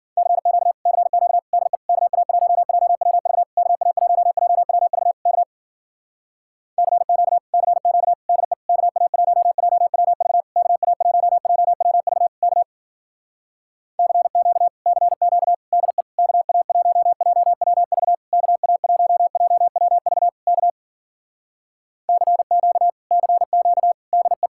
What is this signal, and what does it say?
Morse code, heard as a keyed tone, about 52 wpm, 700 Hz. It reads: CQ CQ DE KA1JWV KA1JWV K CQ CQ DE KA1JWV KA1JWV K CQ CQ DE KA1JWV KA1JWV K CQ CQ DE